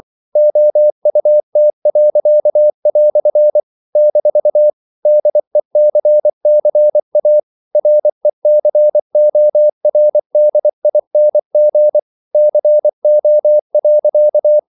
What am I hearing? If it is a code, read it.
Morse code, heard as a keyed tone, about 24 wpm, 600 Hz. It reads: OUT." - DECCA RECORDING CO.
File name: CW / Practice / EnglishQuotations0424